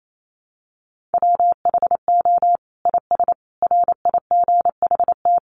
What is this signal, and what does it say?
Morse code, heard as a keyed tone, about 28 wpm, 700 Hz. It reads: WHO IS RIGHT
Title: Sentences from Top 900 Words 28wpm